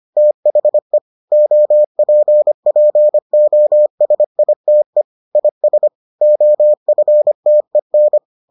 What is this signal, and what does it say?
Morse code, heard as a keyed tone, about 25 wpm, 600 Hz. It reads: THE OPPOSITE IS OFTEN